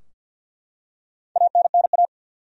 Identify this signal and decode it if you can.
Morse code, heard as a keyed tone, about 51 wpm, 700 Hz. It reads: ANNA
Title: USA Common Names - 50wpm